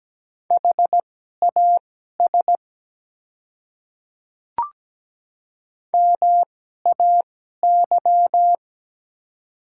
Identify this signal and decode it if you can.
Morse code, heard as a keyed tone, about 17 wpm, spaced as for 12 wpm, 700 Hz. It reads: HAS MAY
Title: 2x Character Spacing - Top 100 Words - 17wpm